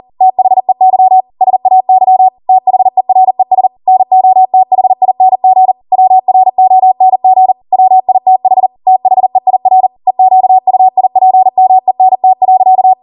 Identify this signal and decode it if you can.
Morse code, heard as a keyed tone, about 40 wpm, 750 Hz. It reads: THEY SAY THERES NOTHING WRONG WITH THEIR EQUIPMENT.
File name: Morse / 40WPM